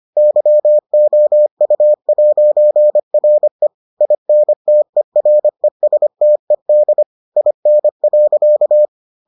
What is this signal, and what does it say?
Morse code, heard as a keyed tone, about 25 wpm, 600 Hz. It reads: YOU'RE INTERESTED IN.